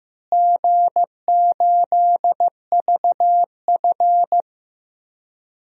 Morse code, heard as a keyed tone, about 16 wpm, 700 Hz. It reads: G8VF
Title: DXCC-100 Call Signs - 4 Characters - Repeated 2x - 15wpm